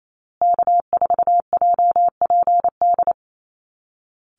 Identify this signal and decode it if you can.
Morse code, heard as a keyed tone, about 28 wpm, 700 Hz. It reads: K4JPD
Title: Call Signs Repeated 2x - 28wpm